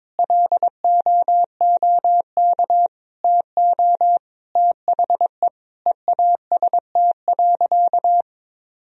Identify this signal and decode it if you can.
Morse code, heard as a keyed tone, about 22 wpm, 700 Hz. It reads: LOOK TO THE EAST.
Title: Sentences from Top 1000 Words - Review Part 3 - 22wpm